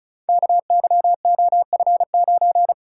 Morse code, heard as a keyed tone, about 33 wpm, 700 Hz. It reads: KYOF9
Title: Code Groups - 5 Letter:Numbers - 35wpm